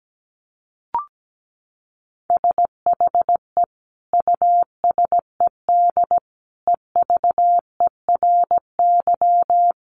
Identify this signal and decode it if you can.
Morse code, heard as a keyed tone, about 17 wpm, 700 Hz. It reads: SHE USED EVERY
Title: Sentences from Top 1400 Words 17wpm